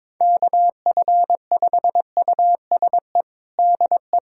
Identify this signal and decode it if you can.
Morse code, heard as a keyed tone, about 22 wpm, 700 Hz. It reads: KF5USE DE
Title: USA Callsign DE Callsign 22wpm